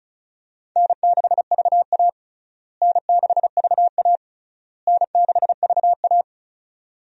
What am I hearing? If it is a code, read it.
Morse code, heard as a keyed tone, about 36 wpm, 700 Hz. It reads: N6VA N6VA N6VA
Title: USA Call Signs - 4 Characters - Repeated 3x - 35wpm